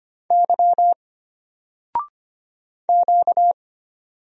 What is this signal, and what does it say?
Morse code, heard as a keyed tone, about 25 wpm, 700 Hz. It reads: Y Q
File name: Single-Letter-Number-25wpm